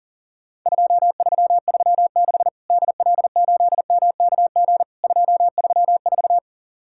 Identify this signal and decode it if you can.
Morse code, heard as a keyed tone, about 40 wpm, 700 Hz. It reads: 2336 DL8MKG 234
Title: Worked All Europe DX Contest - QTC Exchange - 40wpm